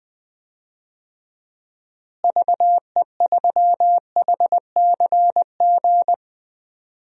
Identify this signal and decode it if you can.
Morse code, heard as a keyed tone, about 20 wpm, 700 Hz. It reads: VE3HCG
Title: DXCC-100 Call Signs - 6 Characters - Repeated 3x - 20wpm